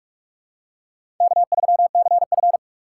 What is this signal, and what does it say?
Morse code, heard as a keyed tone, about 41 wpm, 700 Hz. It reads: K3CF
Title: USA Call Signs - 4 Characters - 3x Character Spacing - 45wpm